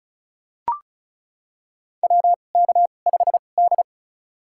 Morse code, heard as a keyed tone, about 36 wpm, spaced as for 25 wpm, 700 Hz. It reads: WK5D
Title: USA Call Signs - 2x Character Spacing - 35wpm